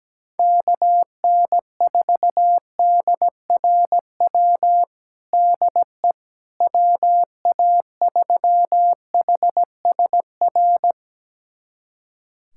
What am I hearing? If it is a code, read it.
Morse code, heard as a keyed tone, about 17 wpm, 700 Hz. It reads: KN4DRW DE WA3HSR
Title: USA Callsign DE Callsign 17wpm